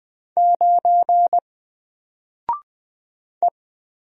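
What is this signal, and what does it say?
Morse code, heard as a keyed tone, about 20 wpm, 700 Hz. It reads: 9 E
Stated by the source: Single letter and number Morse drill at 20 wpm, from N4LCD.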